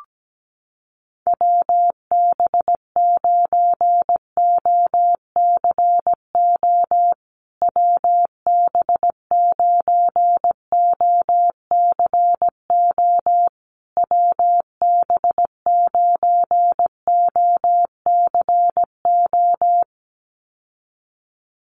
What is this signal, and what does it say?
Morse code, heard as a keyed tone, about 17 wpm, 700 Hz. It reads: WB9OCO WB9OCO WB9OCO
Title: USA Call Signs Repeated 3x 17wpm